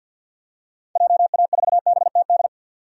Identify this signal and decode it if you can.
Morse code, heard as a keyed tone, about 47 wpm, 700 Hz. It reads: JA4BTD
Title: DXCC-100 Call Signs - 6 Characters - 50wpm